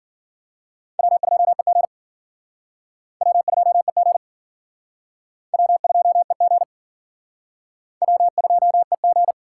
Morse code, heard as a keyed tone, about 53 wpm, 690 Hz. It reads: W2EG W2EG W2EG W2EG